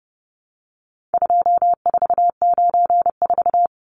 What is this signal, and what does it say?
Morse code, heard as a keyed tone, about 30 wpm, 700 Hz. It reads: 2494
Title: Numbers - 4-Digit 30wpm